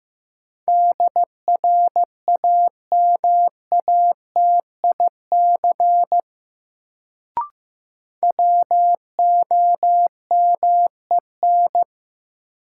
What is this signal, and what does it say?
Morse code, heard as a keyed tone, about 15 wpm, 700 Hz. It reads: DRAMATIC WOMEN
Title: Sentences from Top 1300 Words - New Word Drill - 15wpm